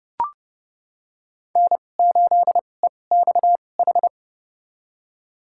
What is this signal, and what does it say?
Morse code, heard as a keyed tone, about 30 wpm, spaced as for 22 wpm, 700 Hz. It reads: N8EXH
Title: USA Call Signs - 5 Characters - 2x Character Spacing - 30wpm